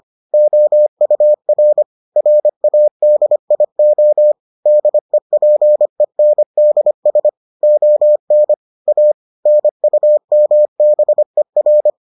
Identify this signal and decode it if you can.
Morse code, heard as a keyed tone, about 25 wpm, 600 Hz. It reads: OUR RADIO DEPENDS ON A NUMBER